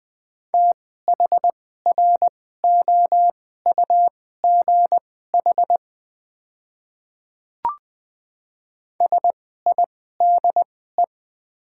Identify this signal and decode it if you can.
Morse code, heard as a keyed tone, about 20 wpm, spaced as for 14 wpm, 690 Hz. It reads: THROUGH SIDE
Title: 2x Character Spacing - Top 300 Words - 20wpm